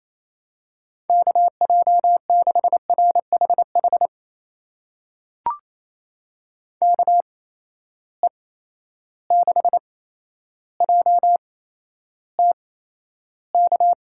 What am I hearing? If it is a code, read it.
Morse code, heard as a keyed tone, about 28 wpm, spaced as for 12 wpm, 700 Hz. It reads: KJ6RHH KE6JTK
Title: USA Call Signs - 6 Characters - 8x Character Spacing - 28wpm